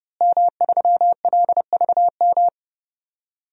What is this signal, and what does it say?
Morse code, heard as a keyed tone, about 30 wpm, 700 Hz. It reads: M3LVM
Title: DXCC-100 Call Signs - 5 Characters - Repeated 2x - 30wpm